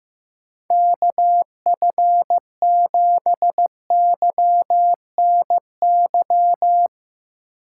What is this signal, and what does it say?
Morse code, heard as a keyed tone, about 17 wpm, 700 Hz. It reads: KF7YNY
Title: Speed Racing - USA 2x3 Call Signs 15wpm